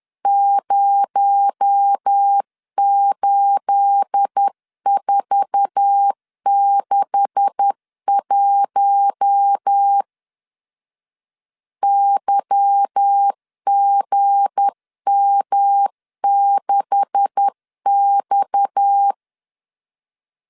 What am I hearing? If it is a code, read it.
Morse code, heard as a keyed tone, about 10 wpm, 800 Hz. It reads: 08461 YGM6X
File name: LEZIONI / 9 286 10wpm 2735